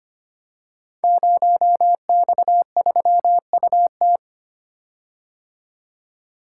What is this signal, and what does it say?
Morse code, heard as a keyed tone, about 25 wpm, 700 Hz. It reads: 0X3UT
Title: Code Groups - 5 Letter:Numbers - 25wpm